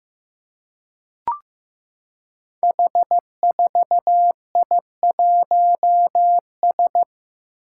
Morse code, heard as a keyed tone, about 15 wpm, 700 Hz. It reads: H4I1S
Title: Five Challenging Letter-Number Combinations 15wpm